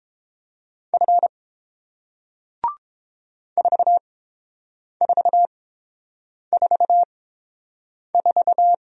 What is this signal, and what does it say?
Morse code, heard as a keyed tone, about 30 wpm, spaced as for 7 wpm, 700 Hz. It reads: F 4444